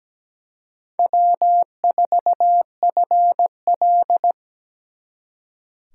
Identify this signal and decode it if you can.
Morse code, heard as a keyed tone, about 20 wpm, 700 Hz. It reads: W4FL
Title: Speed Racing - USA 1x2 Call Signs 17wpm